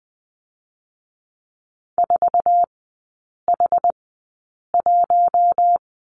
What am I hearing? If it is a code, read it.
Morse code, heard as a keyed tone, about 20 wpm, spaced as for 8 wpm, 700 Hz. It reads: 4H1
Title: 074 Lesson 13e - Letter L - Three Characters 20wpm